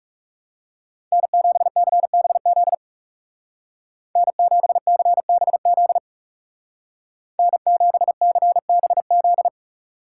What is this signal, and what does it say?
Morse code, heard as a keyed tone, about 40 wpm, 700 Hz. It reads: N7CBZ N7CBZ N7CBZ